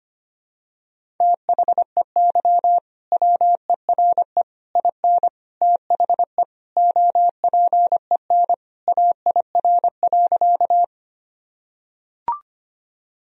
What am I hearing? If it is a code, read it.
Morse code, heard as a keyed tone, about 25 wpm, 700 Hz. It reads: THEY WERE IN THE OPEN AIR.
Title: Sentences from Top 1000 Words - Review Part 1 - 25wpm